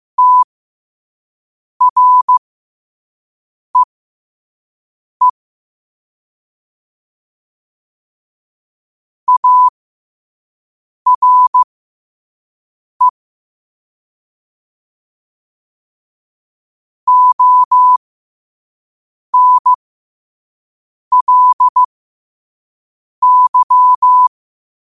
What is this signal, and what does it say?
Morse code, heard as a keyed tone, about 15 wpm, spaced as for 5 wpm, 1000 Hz. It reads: TREE ARE ONLY
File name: qotdfiles / 5wpm